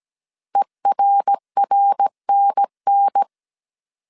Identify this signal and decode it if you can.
Morse code, heard as a keyed tone, about 17 wpm, 800 Hz. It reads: ERRNN